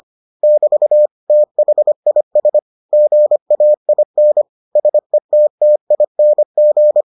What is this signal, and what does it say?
Morse code, heard as a keyed tone, about 25 wpm, 600 Hz. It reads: <BT> THIS GAIN SETTING